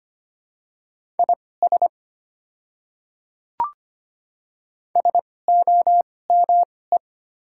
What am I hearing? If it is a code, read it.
Morse code, heard as a keyed tone, about 25 wpm, spaced as for 18 wpm, 690 Hz. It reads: IS SOME
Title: 2x Character Spacing - Top 100 Words - 25wpm